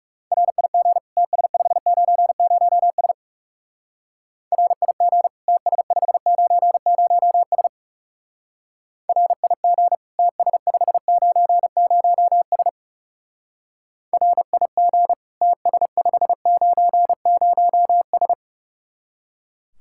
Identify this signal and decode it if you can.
Morse code, heard as a keyed tone, about 38 wpm, 700 Hz. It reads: RIG TS590S RIG TS590S RIG TS590S RIG TS590S